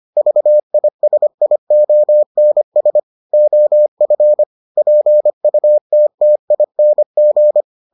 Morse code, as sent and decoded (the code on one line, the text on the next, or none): ...- .. ... .. --- -. ... / --- ..-. / .--. ..- - - .. -. --.
VISIONS OF PUTTING